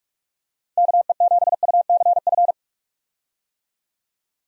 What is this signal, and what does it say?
Morse code, heard as a keyed tone, about 42 wpm, 700 Hz. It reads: KE7UCF
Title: USA Call Signs - 6 Characters - 45wpm